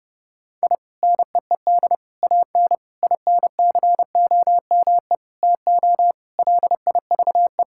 Morse code, heard as a keyed tone, about 30 wpm, 700 Hz. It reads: I NEED AN INCOME TO LIVE
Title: Sentences from Top 1000 Words - Review Part 5 - 30wpm